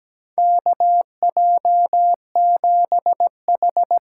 Morse code, heard as a keyed tone, about 17 wpm, 700 Hz. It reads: KJ7H
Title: USA Call Signs - 6 Characters - 8x Character Spacing - 17wpm